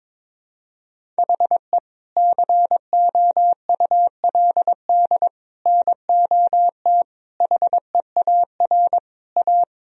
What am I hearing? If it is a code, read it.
Morse code, heard as a keyed tone, about 22 wpm, 700 Hz. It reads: HE COULD NOT HEAR A
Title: Sentences from Top 1000 Words - Review Part 1 - 22wpm